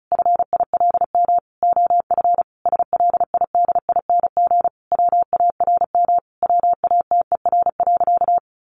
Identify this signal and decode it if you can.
Morse code, heard as a keyed tone, about 34 wpm, 700 Hz. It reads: FILM OF SLIDING WARM WATER.
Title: A Sound Of Thunder 35wpm